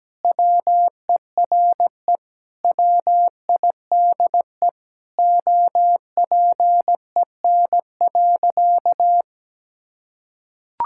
Morse code, heard as a keyed tone, about 17 wpm, 700 Hz. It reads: WERE WIDE OPEN.
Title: Sentences from Top 1000 Words - Review Part 6 - 17wpm